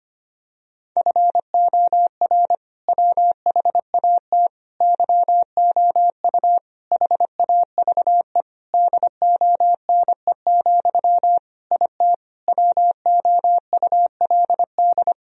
Morse code, heard as a keyed tone, about 25 wpm, 700 Hz. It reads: FOR WHAT YOU HAVE DONE, IT WOULD
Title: Sentences from Top 1000 Words - Review Part 2 - 25wpm